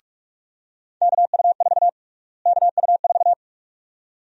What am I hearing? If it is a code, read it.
Morse code, heard as a keyed tone, about 42 wpm, 700 Hz. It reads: KU4 KU4
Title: USA Call Sign Prefixes - Repeated 2x - 45wpm